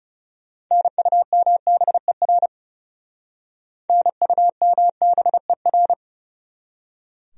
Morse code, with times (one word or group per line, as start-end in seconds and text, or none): 0.7-2.5 NUMBER
3.9-5.9 NUMBER